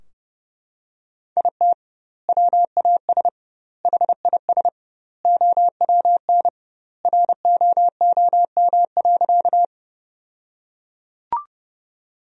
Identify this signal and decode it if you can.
Morse code, heard as a keyed tone, about 30 wpm, 700 Hz. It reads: IT WAS HIS OWN ROOM.
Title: 2x Word-Spacing - Sentences from Top 300 Words - 30wpm